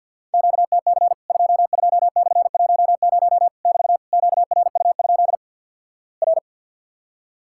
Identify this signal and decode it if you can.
Morse code, heard as a keyed tone, about 50 wpm, 700 Hz. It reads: QTC 22/10 <BT> QRU? R
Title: Worked All Europe DX Contest - QTC Exchange with Repeats - 50wpm